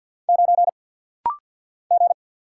Morse code, heard as a keyed tone, about 48 wpm, 700 Hz. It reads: 9 G
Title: Single Letter-Number - Rapid-Fire 50wpm